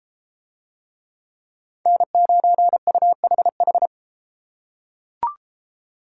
Morse code, heard as a keyed tone, about 31 wpm, 700 Hz. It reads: N9UHH